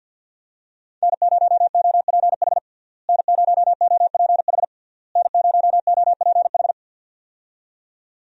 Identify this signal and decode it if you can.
Morse code, heard as a keyed tone, about 49 wpm, 700 Hz. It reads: N0OPH N0OPH N0OPH
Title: USA Call Signs - 5 Characters - Repeated 3x - 50wpm